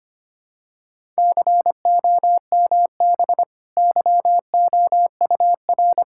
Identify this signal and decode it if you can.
Morse code, heard as a keyed tone, about 25 wpm, 700 Hz. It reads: COMB YOUR
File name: Easy Sentences with Hints - 25wpm